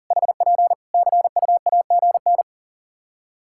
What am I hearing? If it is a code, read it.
Morse code, heard as a keyed tone, about 40 wpm, 700 Hz. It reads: HP CUAGN
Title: Practice QSO - Exchange 4 - Ending - 40wpm